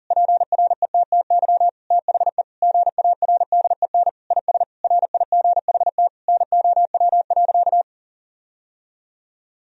Morse code, heard as a keyed tone, about 40 wpm, 700 Hz. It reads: PRETTY THE GARDEN IS RIGHT NOW.
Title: Sentences from Top 1000 Words - Review Part 4 - 40wpm